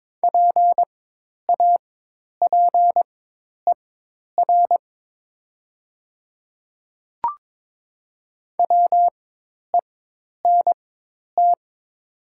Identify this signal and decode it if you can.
Morse code, heard as a keyed tone, about 22 wpm, spaced as for 10 wpm, 700 Hz. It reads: PAPER WENT
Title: 4x Character Spacing - Top 300 Words - 22wpm